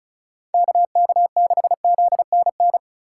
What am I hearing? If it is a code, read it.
Morse code, heard as a keyed tone, about 34 wpm, 700 Hz. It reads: KK6ZNN
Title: USA Call Signs - 6 Characters - 3x Character Spacing - 35wpm